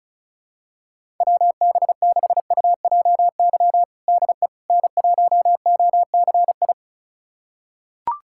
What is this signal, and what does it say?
Morse code, heard as a keyed tone, about 34 wpm, 700 Hz. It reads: WB6UJY DE N1OCI
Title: USA Callsign DE Callsign 35wpm